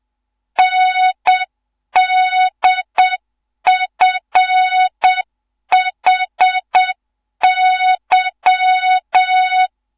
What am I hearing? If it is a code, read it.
Morse code, heard as a keyed tone, about 7 wpm, 3700 Hz. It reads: NDFHY
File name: Morze-HA2MN / hang / 15 GYAKORLAT